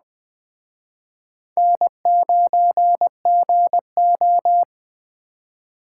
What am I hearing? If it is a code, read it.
Morse code, heard as a keyed tone, about 20 wpm, 700 Hz. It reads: N9GO
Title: Speed Racing - USA Call Signs 17wpm